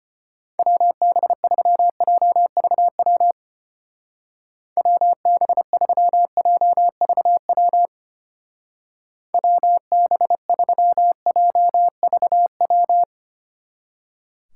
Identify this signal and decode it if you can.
Morse code, heard as a keyed tone, about 31 wpm, 700 Hz. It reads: WB3JVW WB3JVW WB3JVW
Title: Speed Racing - USA Call Signs 25wpm